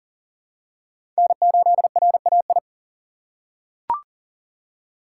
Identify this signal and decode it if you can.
Morse code, heard as a keyed tone, about 42 wpm, 700 Hz. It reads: N8RAI
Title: USA Call Signs - 6x Character Spacing - 40wpm